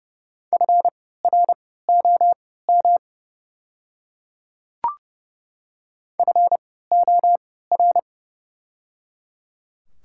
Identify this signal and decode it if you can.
Morse code, heard as a keyed tone, about 30 wpm, spaced as for 17 wpm, 700 Hz. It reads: FROM FOR
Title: 3x Character Spacing - Top 200 Words - 30wpm